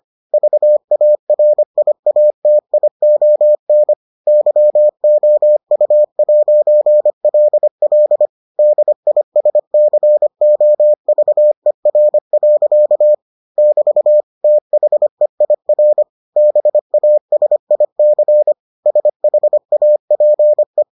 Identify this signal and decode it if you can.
Morse code, heard as a keyed tone, about 25 wpm, 600 Hz. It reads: VARIATION YOU'LL DISCOVER. <BT> THEIR BASIC SHAPE